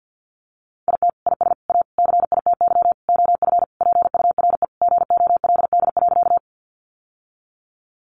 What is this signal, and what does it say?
Morse code, heard as a keyed tone, about 50 wpm, 700 Hz. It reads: IT IS A CITY OF PURE GOLD.
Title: Sentences from Top 1500 Words 50wpm